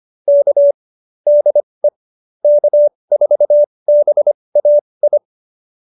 Morse code, heard as a keyed tone, about 25 wpm, spaced as for 20 wpm, 570 Hz. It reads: K DE K4BAI
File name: QSO301 20